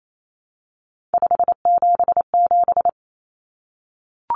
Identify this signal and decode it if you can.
Morse code, heard as a keyed tone, about 28 wpm, 700 Hz. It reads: 577
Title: Common Cut Numbers - 3-Digit 28wpm